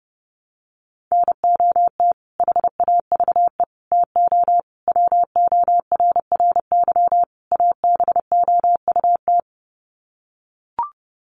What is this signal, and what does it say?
Morse code, heard as a keyed tone, about 30 wpm, 700 Hz. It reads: NOT HAVE TO WORRY ABOUT
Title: Sets of 5 Words 30wpm